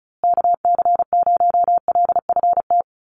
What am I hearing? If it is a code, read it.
Morse code, heard as a keyed tone, about 34 wpm, 700 Hz. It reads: KC0LFT
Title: USA Call Signs Repeated 2x 35wpm